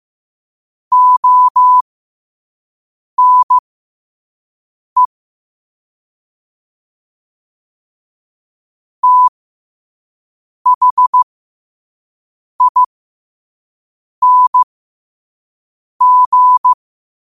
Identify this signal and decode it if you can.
Morse code, heard as a keyed tone, about 15 wpm, spaced as for 5 wpm, 1000 Hz. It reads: ONE THING